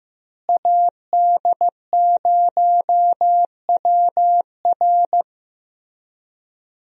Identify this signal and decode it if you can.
Morse code, heard as a keyed tone, about 15 wpm, 700 Hz. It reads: AD0WR
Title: USA Call Signs - 5 Characters - Repeated 3x - 15wpm